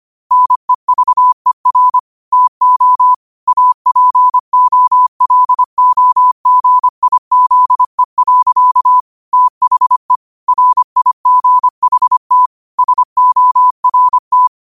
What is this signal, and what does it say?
Morse code, heard as a keyed tone, about 25 wpm, 1000 Hz. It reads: NEVER TO APOLOGIZE. THE RIGHT SORT